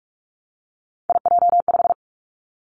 Morse code, heard as a keyed tone, about 46 wpm, 700 Hz. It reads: IJ5
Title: Three Challenging Letter-Number Combinations 45wpm